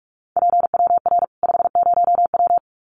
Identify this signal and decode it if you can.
Morse code, heard as a keyed tone, about 44 wpm, 700 Hz. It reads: PWR 50W
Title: Common QSO Phrases 45wpm